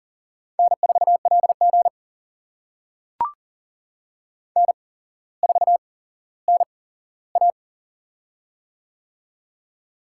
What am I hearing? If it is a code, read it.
Morse code, heard as a keyed tone, about 40 wpm, spaced as for 17 wpm, 700 Hz. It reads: N4LG N4NA
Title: USA Call Signs - 4 Characters - 8x Character Spacing - 40wpm